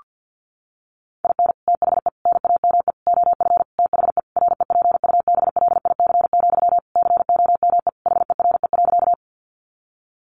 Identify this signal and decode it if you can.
Morse code, heard as a keyed tone, about 50 wpm, 700 Hz. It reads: IN THE NAME OF THE REPUBLIC, COME HERE.
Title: Sentences from Top 1500 Words 50wpm